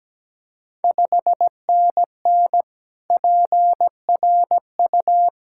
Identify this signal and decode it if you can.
Morse code, heard as a keyed tone, about 17 wpm, 700 Hz. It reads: 5NN PRU
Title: Ontario QSO Party Exchange 17wpm